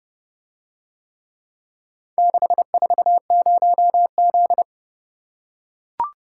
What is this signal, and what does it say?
Morse code, heard as a keyed tone, about 29 wpm, 700 Hz. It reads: 640Z